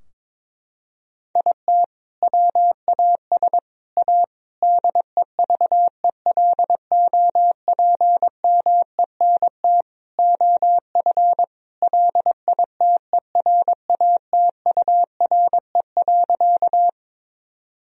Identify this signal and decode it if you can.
Morse code, heard as a keyed tone, about 22 wpm, 700 Hz. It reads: IT WAS A DEVELOPMENT OF LITERATURE.
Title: Sentences from Top 1000 Words - Review Part 4 - 22wpm